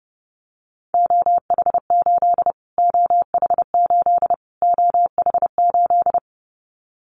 Morse code, heard as a keyed tone, about 28 wpm, 700 Hz. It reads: OH8 OH8 OH8
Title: Callsign Prefixes for Top 25 DXCC - Repeated 3x - 30wpm